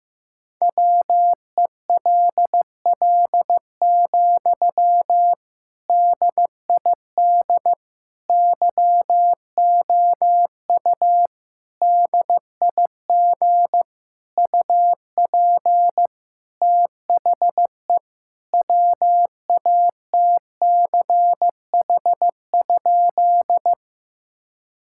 Morse code, heard as a keyed tone, about 15 wpm, 700 Hz. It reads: WELL, DID YOU DIG UP THE WATCH?
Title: Sentences from Top 1600 Words 15wpm